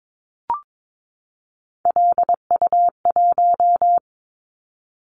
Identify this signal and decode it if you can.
Morse code, heard as a keyed tone, about 22 wpm, 700 Hz. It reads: LU1
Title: Callsign Prefixes for Top 50 DXCC - 22wpm